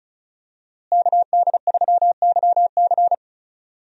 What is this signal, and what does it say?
Morse code, heard as a keyed tone, about 36 wpm, 700 Hz. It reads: KD3YC
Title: USA Call Signs - 5 Characters - Repeated 3x - 35wpm